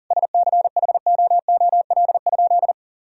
Morse code, heard as a keyed tone, about 37 wpm, 700 Hz. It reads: SCHOOL?
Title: Sentences from Top 1000 Words - Review Part 3 - 40wpm